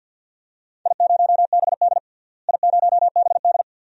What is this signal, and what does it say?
Morse code, heard as a keyed tone, about 47 wpm, 700 Hz. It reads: I0BD I0BD
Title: DXCC-100 Call Signs - 4 Characters - Repeated 2x - 50wpm